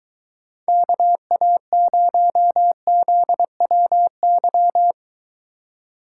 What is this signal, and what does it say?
Morse code, heard as a keyed tone, about 23 wpm, 700 Hz. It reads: KA0ZWY